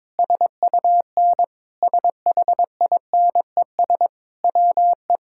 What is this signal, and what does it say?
Morse code, heard as a keyed tone, about 21 wpm, 700 Hz. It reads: SUN SHINES WE
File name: practice / 22wpm / Sentences with 4-Word Phrases - 22wpm